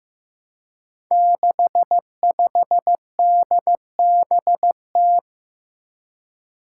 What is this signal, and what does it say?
Morse code, heard as a keyed tone, about 15 wpm, 700 Hz. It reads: 65DBT
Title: Five Challenging Letter-Number Combinations 15wpm